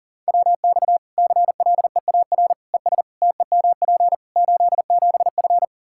Morse code, heard as a keyed tone, about 38 wpm, 700 Hz. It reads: WX CLEAR ES TEMP 87F
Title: Practice QSO - Exchange 2 - Info - 40wpm